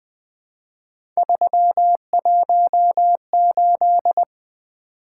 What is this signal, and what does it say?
Morse code, heard as a keyed tone, about 20 wpm, 700 Hz. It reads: 318
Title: Three Letter-Number Combinations - 20wpm